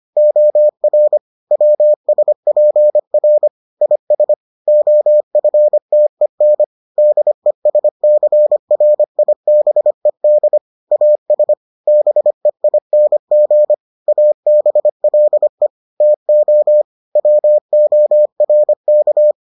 Morse code, heard as a keyed tone, about 25 wpm, 600 Hz. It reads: OR WSPR IS OFTEN DESCRIBED AS BEING ABLE TO WORK